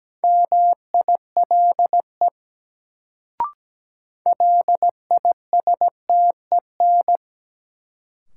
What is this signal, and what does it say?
Morse code, heard as a keyed tone, about 17 wpm, 700 Hz. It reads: MILE LISTEN
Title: Sentences from Top 400 Words - New Word Drill - 17wpm